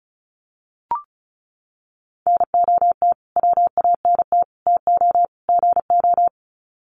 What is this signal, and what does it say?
Morse code, heard as a keyed tone, about 34 wpm, 700 Hz. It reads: NOT WANT TO GO
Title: Sets of 4 Words 35wpm